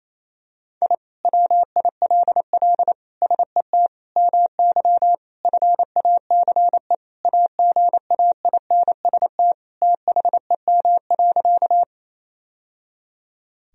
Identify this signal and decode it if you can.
Morse code, heard as a keyed tone, about 27 wpm, 700 Hz. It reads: I WILL SET MY FACE AGAINST THEM.
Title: Sentences from Top 1000 Words - Review Part 2 - 28wpm